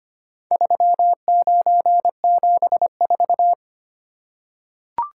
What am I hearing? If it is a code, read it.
Morse code, heard as a keyed tone, about 25 wpm, 700 Hz. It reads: 3974
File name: practice / 25wpm / Four Letter-Number Combinations - 25wpm